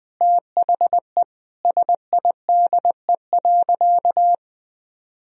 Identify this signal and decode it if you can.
Morse code, heard as a keyed tone, about 19 wpm, 700 Hz. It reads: THE SIDE.
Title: Sentences from Top 1000 Words - Review Part 4 - 20wpm